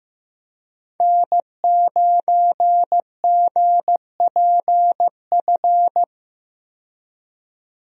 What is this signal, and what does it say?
Morse code, heard as a keyed tone, about 15 wpm, 700 Hz. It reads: N9GPF
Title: USA Call Signs - 5 Characters - 15wpm